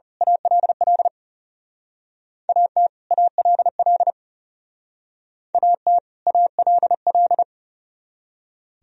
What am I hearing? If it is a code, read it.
Morse code, heard as a keyed tone, about 34 wpm, 700 Hz. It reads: ALL AT ALL AT ALL
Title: Speed-Racing - Sets of 2 Words 30wpm